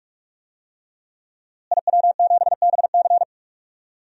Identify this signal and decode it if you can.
Morse code, heard as a keyed tone, about 46 wpm, 700 Hz. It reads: IW7BC
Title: DXCC-100 Call Signs - 5 Characters - Repeated 3x - 45wpm